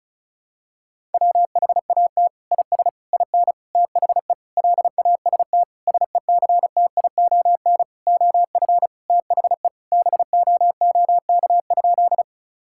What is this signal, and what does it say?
Morse code, heard as a keyed tone, about 36 wpm, 700 Hz. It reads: WHAT IS IN THE LAST SECTION OF THE BOOK?
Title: Sentences from Top 1000 Words - Review Part 5 - 35wpm